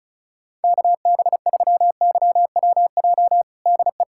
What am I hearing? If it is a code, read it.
Morse code, heard as a keyed tone, about 36 wpm, 700 Hz. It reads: KB3YWJ DE
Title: USA Callsign DE Callsign 35wpm